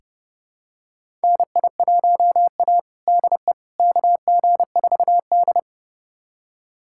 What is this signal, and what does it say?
Morse code, heard as a keyed tone, about 30 wpm, 700 Hz. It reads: NI1A DE KG4D
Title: USA Callsign DE Callsign - 4 Characters - 30wpm